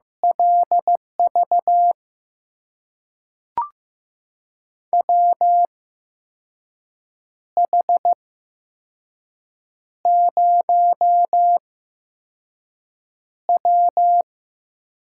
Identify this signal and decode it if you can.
Morse code, heard as a keyed tone, about 15 wpm, spaced as for 5 wpm, 700 Hz. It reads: LV WH0W
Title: USA Call Signs - 8x Character Spacing - 15wpm